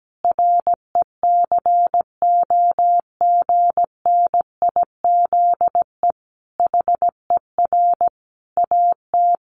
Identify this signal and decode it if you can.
Morse code, heard as a keyed tone, about 17 wpm, 700 Hz. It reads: RECOGNIZE HER AT
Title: Sentences from Top 1400 Words 17wpm